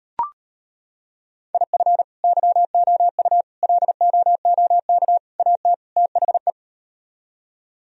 Morse code, heard as a keyed tone, about 38 wpm, 700 Hz. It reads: IF YOU LOOK AT THE